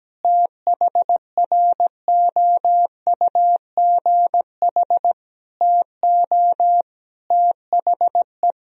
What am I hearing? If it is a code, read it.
Morse code, heard as a keyed tone, about 17 wpm, 700 Hz. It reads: THROUGH TO THE
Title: Sentences from Top 1000 Words - Review Part 1 - 17wpm